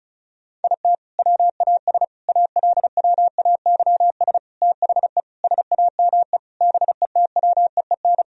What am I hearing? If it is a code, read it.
Morse code, heard as a keyed tone, about 36 wpm, 700 Hz. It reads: IT WAS ALWAYS THE SAME BETWEEN
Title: Sentences from Top 1000 Words - Review Part 1 - 35wpm